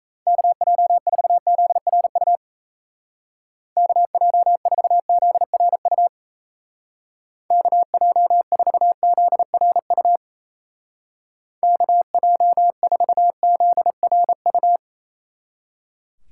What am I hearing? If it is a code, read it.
Morse code, heard as a keyed tone, about 34 wpm, 700 Hz. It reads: KJ4ZRU KJ4ZRU KJ4ZRU KJ4ZRU
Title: Speed Racing - USA 2x3 Call Signs 28wpm